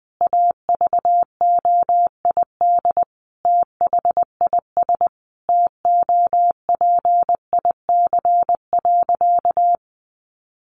Morse code, heard as a keyed tone, about 20 wpm, 700 Hz. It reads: AVOID THIS TOPIC.
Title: Sentences from Top 1200 Words 20wpm